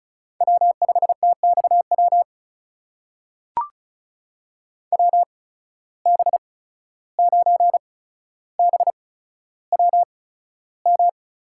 Morse code, heard as a keyed tone, about 36 wpm, spaced as for 14 wpm, 700 Hz. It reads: W5TXW WB9BWM
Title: USA Call Signs - 8x Character Spacing - 35wpm